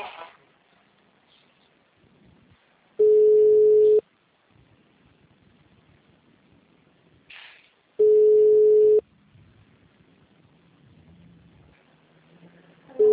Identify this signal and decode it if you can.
Ringback tone (Europe, 425 Hz).
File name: out- -2017- - - .20